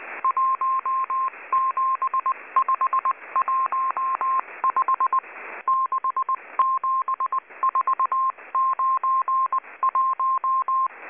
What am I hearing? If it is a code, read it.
Morse code, heard as a keyed tone, about 19 wpm, 1060 Hz. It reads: 17515 67491
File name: HM02- -0540utc